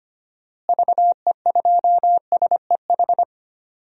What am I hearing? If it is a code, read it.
Morse code, heard as a keyed tone, about 25 wpm, 700 Hz. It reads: VE2SEH
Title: DXCC-100 Call Signs - 6 Characters - Repeated 2x - 25wpm